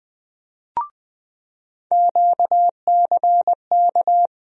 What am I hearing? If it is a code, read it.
Morse code, heard as a keyed tone, about 20 wpm, 700 Hz. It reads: QCK
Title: USA Call Sign Postfixes - 3 Letters 20wpm